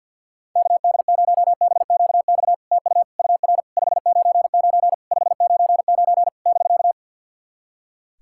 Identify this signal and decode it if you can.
Morse code, heard as a keyed tone, about 50 wpm, 700 Hz. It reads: KD0BQX TU UR 599 599 <BK>
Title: SOTA QSO Practice 50wpm